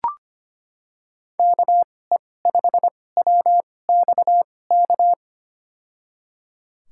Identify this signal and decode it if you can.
Morse code, heard as a keyed tone, about 25 wpm, spaced as for 18 wpm, 700 Hz. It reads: KE5WXK
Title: USA Call Signs - 6 Characters - 2x Character Spacing - 25wpm